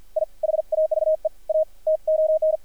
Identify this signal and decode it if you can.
Morse code, heard as a keyed tone, about 53 wpm, 590 Hz. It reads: I HAVE A TON
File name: archived / CRSnet